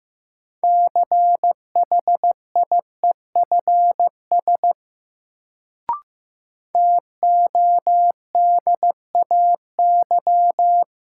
Morse code, heard as a keyed tone, about 15 wpm, 700 Hz. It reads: CHIEFS TODAY
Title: Sets of 5 Words Encore - New Word Drill - 15wpm